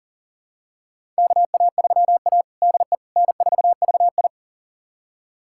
Morse code, heard as a keyed tone, about 40 wpm, 700 Hz. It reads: KA3A DE N4VI
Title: USA Callsign DE Callsign - 4 Characters - 40wpm